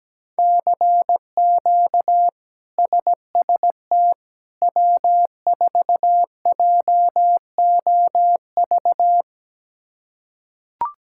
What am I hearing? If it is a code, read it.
Morse code, heard as a keyed tone, about 17 wpm, 700 Hz. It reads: CQ SST W4JOV
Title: K1USN SST QSO Practice - 17wpm